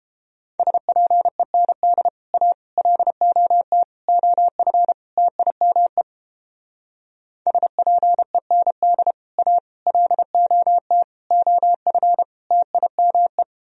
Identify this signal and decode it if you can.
Morse code, heard as a keyed tone, about 29 wpm, 700 Hz. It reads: SPEND A LOT OF TIME SPEND A LOT OF TIME